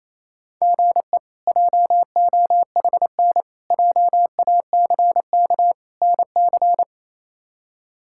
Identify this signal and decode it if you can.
Morse code, heard as a keyed tone, about 28 wpm, 700 Hz. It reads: GE JOHN JACK NC
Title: K1USN SST QSO Practice - 28wpm